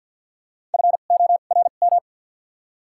Morse code, heard as a keyed tone, about 47 wpm, spaced as for 35 wpm, 700 Hz. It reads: FORM
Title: 2x Character Spacing - Top 300 Words - 50wpm